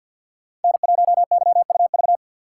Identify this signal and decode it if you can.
Morse code, heard as a keyed tone, about 52 wpm, 700 Hz. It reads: N1YUV
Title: USA Call Signs - 5 Characters - 2x Character Spacing - 50wpm